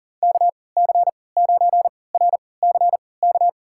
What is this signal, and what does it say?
Morse code, heard as a keyed tone, about 40 wpm, spaced as for 23 wpm, 700 Hz. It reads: KC9RCK
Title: USA Call Signs - 6 Characters - 3x Character Spacing - 40wpm